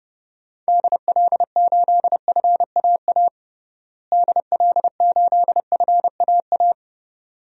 Morse code, heard as a keyed tone, about 30 wpm, 700 Hz. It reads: DL8FAA DL8FAA
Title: DXCC-100 Call Signs - 6 Characters - Repeated 2x - 30wpm